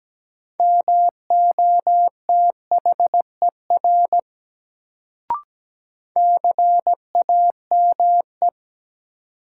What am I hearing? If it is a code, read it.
Morse code, heard as a keyed tone, about 17 wpm, 700 Hz. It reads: MOTHER CAME
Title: Sentences from Top 200 Words - New Word Drill - 17wpm